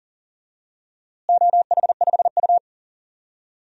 Morse code, heard as a keyed tone, about 40 wpm, 700 Hz. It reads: OH5U
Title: DXCC-100 Call Signs - 4 Characters - Repeated 2x - 40wpm